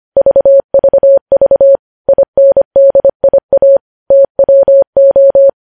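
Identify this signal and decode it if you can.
Morse code, heard as a keyed tone, about 25 wpm, 550 Hz. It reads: VVV INDIA TWO
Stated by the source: Morse news headlines sent at 25 wpm.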